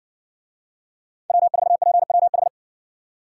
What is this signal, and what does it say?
Morse code, heard as a keyed tone, about 60 wpm, 700 Hz. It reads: W4PWH